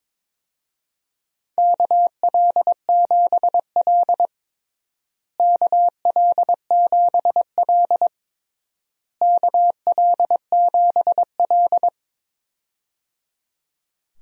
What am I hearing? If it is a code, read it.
Morse code, heard as a keyed tone, about 21 wpm, 700 Hz. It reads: KL7L KL7L KL7L
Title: DXCC-100 Call Signs - 4 Characters - Repeated 3x - 22wpm